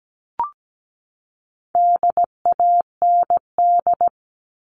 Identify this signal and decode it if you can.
Morse code, heard as a keyed tone, about 17 wpm, 700 Hz. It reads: DAND
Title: All Cut Numbers - 4-Digit 17wpm